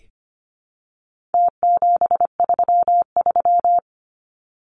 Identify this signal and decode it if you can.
Morse code, heard as a keyed tone, about 25 wpm, 700 Hz. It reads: T733
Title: Numbers - Contest Style - 4-Digit 25wpm